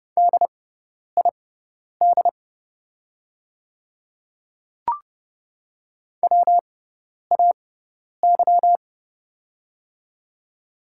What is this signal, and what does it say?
Morse code, heard as a keyed tone, about 30 wpm, spaced as for 10 wpm, 690 Hz. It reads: DID WAY
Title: 6x Character Spacing - Top 200 Words - 30wpm